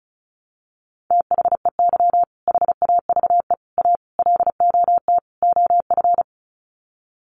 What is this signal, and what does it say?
Morse code, heard as a keyed tone, about 34 wpm, 700 Hz. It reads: THEY HAVE A LOT OF
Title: Sets of 5 Words Encore2 35wpm